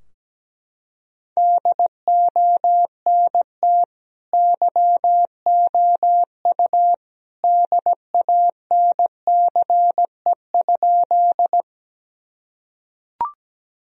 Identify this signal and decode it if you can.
Morse code, heard as a keyed tone, about 17 wpm, 700 Hz. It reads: DONT YOU DANCE?
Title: Sentences from Top 1000 Words - Review Part 4 - 17wpm